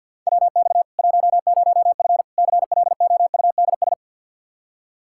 Morse code, heard as a keyed tone, about 46 wpm, 700 Hz. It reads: WX 10F CLOUDS
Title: Practice QSO - Exchange 2 - Info - 50wpm